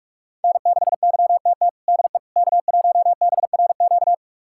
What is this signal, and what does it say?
Morse code, heard as a keyed tone, about 46 wpm, 700 Hz. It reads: N6YTT DE K1BRQ
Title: USA Callsign DE Callsign 45wpm